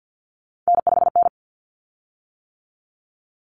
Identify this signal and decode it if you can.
Morse code, heard as a keyed tone, about 50 wpm, 700 Hz. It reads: N5N
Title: Common Cut Numbers - 3-Digit 50wpm